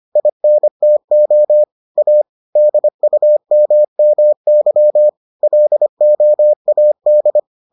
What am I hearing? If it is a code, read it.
Morse code, heard as a keyed tone, about 25 wpm, 600 Hz. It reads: INTO A DUMMY LOAD